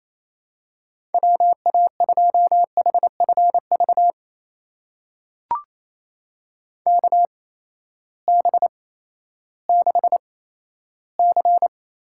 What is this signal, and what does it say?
Morse code, heard as a keyed tone, about 27 wpm, spaced as for 14 wpm, 700 Hz. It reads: WA2HFV KB6C
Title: USA Call Signs - 6 Characters - 8x Character Spacing - 28wpm